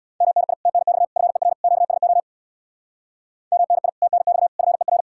Morse code, heard as a keyed tone, about 76 wpm, 700 Hz. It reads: MNI TNX FER QSO MNI TNX FER